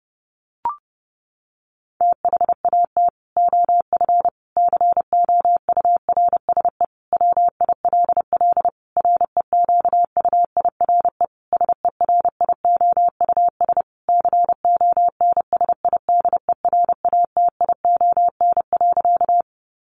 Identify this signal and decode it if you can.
Morse code, heard as a keyed tone, about 30 wpm, 700 Hz. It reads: THAT OF COURSE WILL REQUIRE SERIOUS CONSIDERATION.
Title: Sentences from Top 1500 Words 30wpm